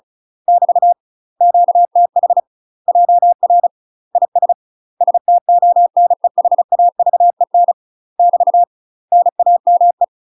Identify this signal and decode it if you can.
Morse code, heard as a keyed tone, about 36 wpm, 700 Hz. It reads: <BT> QTH JR IS STONEHAVEN <BT> NAME
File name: UK MASTER QSOs TRAINING / 35WPM / UK MASTER QSOs 0013